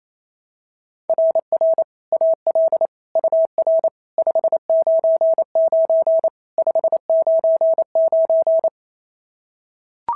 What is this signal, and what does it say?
Morse code, heard as a keyed tone, about 28 wpm, 690 Hz. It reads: RR AL UR 599 599
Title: SOTA QSO Practice 28wpm